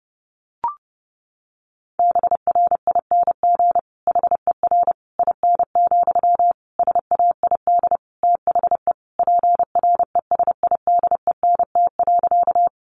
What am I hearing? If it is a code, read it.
Morse code, heard as a keyed tone, about 30 wpm, 700 Hz. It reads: BRING HER IN, SAID THE PRESIDENT.
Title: Sentences from Top 1000 Words 30wpm